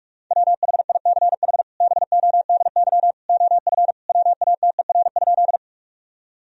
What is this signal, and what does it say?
Morse code, heard as a keyed tone, about 46 wpm, 700 Hz. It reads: WHICH BODY OF WATER?
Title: Sentences from Top 1000 Words - Review Part 2 - 45wpm